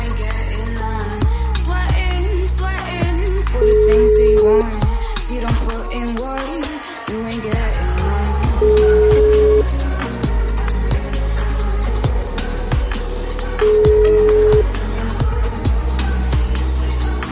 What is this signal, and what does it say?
Ringback tone (Europe, 425 Hz).